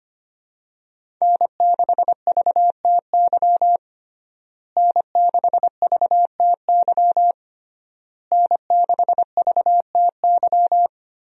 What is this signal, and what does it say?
Morse code, heard as a keyed tone, about 25 wpm, 700 Hz. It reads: N6VTY N6VTY N6VTY
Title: USA Call Signs - 5 Characters - Repeated 3x - 25wpm